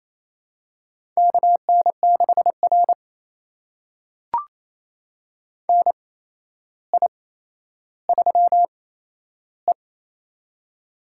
Morse code, heard as a keyed tone, about 28 wpm, spaced as for 12 wpm, 700 Hz. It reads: KN6R NI3E
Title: USA Call Signs - 4 Characters - 8x Character Spacing - 28wpm